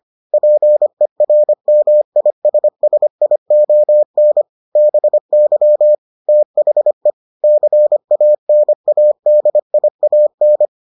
Morse code, heard as a keyed tone, about 25 wpm, 600 Hz. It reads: PERMISSION BY THE CANADIAN